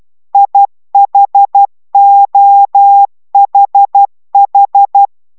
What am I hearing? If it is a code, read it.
Morse code, heard as a keyed tone, about 12 wpm, 800 Hz. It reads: IHOHH